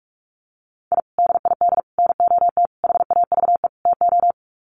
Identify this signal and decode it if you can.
Morse code, heard as a keyed tone, about 44 wpm, 700 Hz. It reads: I DID NOT HAVE TO
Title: Sets of 5 Words 45wpm